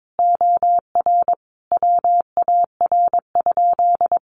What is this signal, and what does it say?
Morse code, heard as a keyed tone, about 22 wpm, 700 Hz. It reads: OR WAR?
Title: Sentences from Top 1500 Words 22wpm